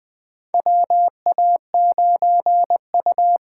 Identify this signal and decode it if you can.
Morse code, heard as a keyed tone, about 20 wpm, 700 Hz. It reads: WA9U
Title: USA Call Signs - 6 Characters - 8x Character Spacing - 20wpm